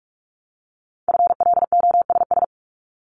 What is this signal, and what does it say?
Morse code, heard as a keyed tone, about 46 wpm, 700 Hz. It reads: FLOSS
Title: Multi-letter Phonemes 45wpm